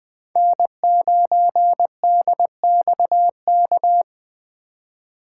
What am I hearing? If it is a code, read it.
Morse code, heard as a keyed tone, about 20 wpm, 700 Hz. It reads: N9DXK
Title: USA Call Signs - 5 Characters - 3x Character Spacing - 20wpm